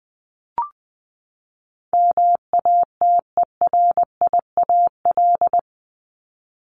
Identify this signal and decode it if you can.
Morse code, heard as a keyed tone, about 20 wpm, 700 Hz. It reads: MATERIAL
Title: ES:S Suffix - Top 100 Words with Base Words 20wpm